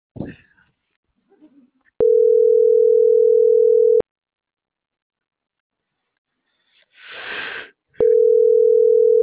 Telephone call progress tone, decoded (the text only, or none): ringback tone